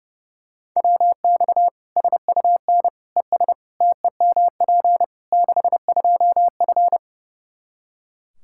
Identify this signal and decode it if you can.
Morse code, heard as a keyed tone, about 30 wpm, 700 Hz. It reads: WX SUN ES TEMP 62F
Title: Practice QSO - Exchange 2 - Info - 30wpm